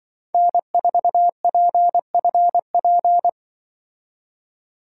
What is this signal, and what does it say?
Morse code, heard as a keyed tone, about 25 wpm, 700 Hz. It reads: N4PFP
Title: Speed Racing - USA Call Signs 20wpm